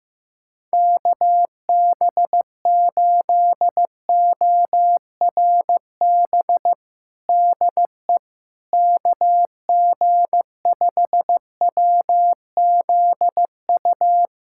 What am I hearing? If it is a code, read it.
Morse code, heard as a keyed tone, about 15 wpm, 700 Hz. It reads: KB8ORB DE KG5WZU
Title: USA Callsign DE Callsign - 6 Characters - 15wpm